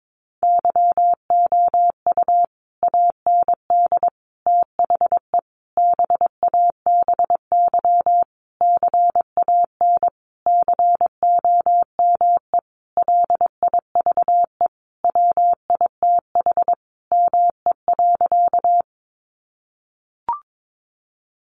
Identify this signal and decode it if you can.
Morse code, heard as a keyed tone, about 22 wpm, 700 Hz. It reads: YOU AND THE BABY CAN COME LIVE WITH ME.
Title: Sentences from Top 1300 Words 22wpm